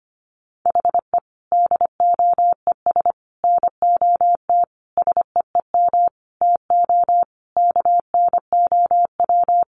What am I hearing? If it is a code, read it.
Morse code, heard as a keyed tone, about 25 wpm, 700 Hz. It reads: HE DOES NOT SEEM TO KNOW
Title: Sentences from Top 1400 Words 25wpm